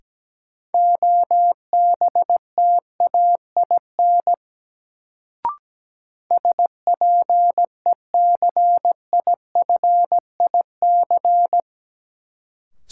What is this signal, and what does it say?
Morse code, heard as a keyed tone, about 17 wpm, 700 Hz. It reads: OBTAIN SPECIFIC
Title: Sentences from Top 1300 Words - New Word Drill Alt - 17wpm